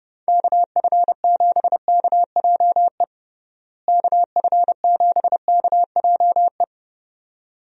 Morse code, heard as a keyed tone, about 32 wpm, 700 Hz. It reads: KF7KJE KF7KJE
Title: USA Call Signs - 6 Characters - Repeated 2x - 30wpm